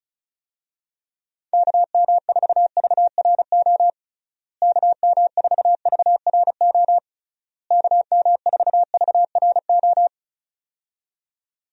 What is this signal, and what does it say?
Morse code, heard as a keyed tone, about 36 wpm, 700 Hz. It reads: KM4VRO KM4VRO KM4VRO
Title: USA Call Signs - 6 Characters - Repeated 3x - 35wpm